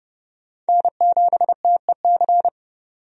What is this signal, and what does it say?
Morse code, heard as a keyed tone, about 30 wpm, 700 Hz. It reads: N7TEC
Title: USA Call Signs - 5 Characters - 30wpm